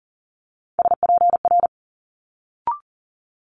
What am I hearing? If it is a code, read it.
Morse code, heard as a keyed tone, about 41 wpm, 700 Hz. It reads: SPR
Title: Multi-letter Phonemes Quick Fire 40wpm